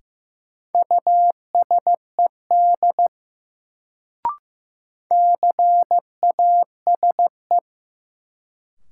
Morse code, heard as a keyed tone, about 15 wpm, 700 Hz. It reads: USED CASE
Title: Sets of 2 Words - New Word Drill - 15wpm